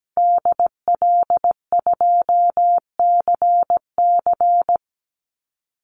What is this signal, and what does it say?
Morse code, heard as a keyed tone, about 17 wpm, 700 Hz. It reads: DL2CC
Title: Call Signs Repeated 2x - 17wpm